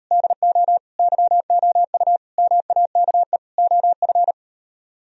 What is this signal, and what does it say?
Morse code, heard as a keyed tone, about 36 wpm, 700 Hz. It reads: DO YOU MAKE OF